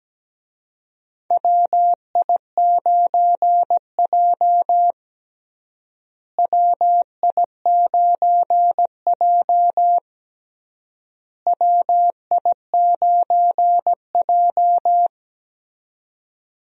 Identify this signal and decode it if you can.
Morse code, heard as a keyed tone, about 17 wpm, 700 Hz. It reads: WI9J WI9J WI9J
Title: USA Call Signs - 4 Characters - Repeated 3x - 17wpm